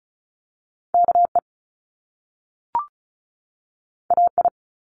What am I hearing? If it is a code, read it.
Morse code, heard as a keyed tone, about 36 wpm, 700 Hz. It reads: KE AI
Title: Top Two Letter Combinations in Words 35wpm